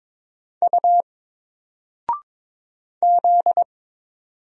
Morse code, heard as a keyed tone, about 22 wpm, 700 Hz. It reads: U Z
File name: Single-Letter-Number-22wpm